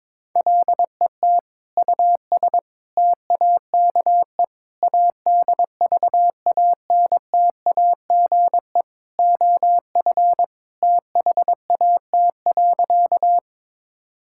Morse code, heard as a keyed tone, about 22 wpm, 700 Hz. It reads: LET US TAKE ADVANTAGE OF THAT.
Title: Sentences from Top 1000 Words - Review Part 6 - 22wpm